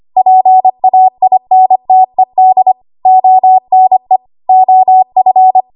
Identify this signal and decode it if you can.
Morse code, heard as a keyed tone, about 25 wpm, 750 Hz. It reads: PAINTED ONE OF
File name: Archive / 25 WPM / 25WPM